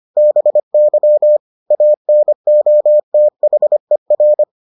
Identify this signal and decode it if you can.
Morse code, heard as a keyed tone, about 25 wpm, 600 Hz. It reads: BY ANOTHER